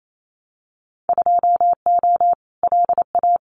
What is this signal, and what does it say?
Morse code, heard as a keyed tone, about 28 wpm, 700 Hz. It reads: 2O LA
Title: Winter Field Day Exchanges 28wpm